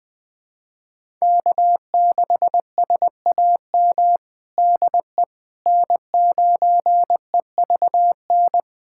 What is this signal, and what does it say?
Morse code, heard as a keyed tone, about 20 wpm, 700 Hz. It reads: K6SAM DE N9EVN
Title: USA Callsign DE Callsign - 5 Characters - 20wpm